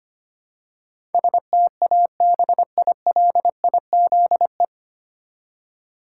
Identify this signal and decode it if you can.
Morse code, heard as a keyed tone, about 25 wpm, 700 Hz. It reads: STABILIZE
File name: practice / 25wpm / 9 Letter Words - First 3 Letters Given - 25wpm